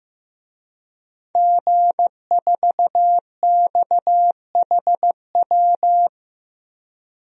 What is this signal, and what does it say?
Morse code, heard as a keyed tone, about 15 wpm, 700 Hz. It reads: G4XHW
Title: DXCC-100 Call Signs - 5 Characters - Repeated 2x - 15wpm